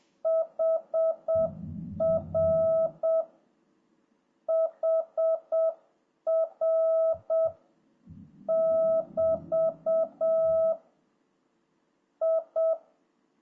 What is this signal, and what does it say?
Morse code, heard as a keyed tone, about 7 wpm, 650 Hz. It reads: HR HR <BT> I